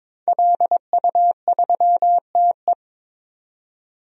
Morse code, heard as a keyed tone, about 22 wpm, 700 Hz. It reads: LU3TE
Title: DXCC-100 Call Signs - 5 Characters - 22wpm